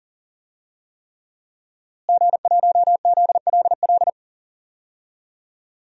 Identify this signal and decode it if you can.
Morse code, heard as a keyed tone, about 40 wpm, 700 Hz. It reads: G1ZLL
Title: DXCC-100 Call Signs - 5 Characters - Repeated 3x - 40wpm